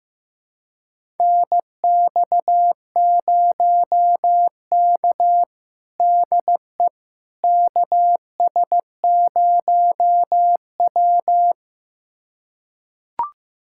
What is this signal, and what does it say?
Morse code, heard as a keyed tone, about 15 wpm, 700 Hz. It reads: NX0K DE KS0W
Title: USA Callsign DE Callsign - 4 Characters - 15wpm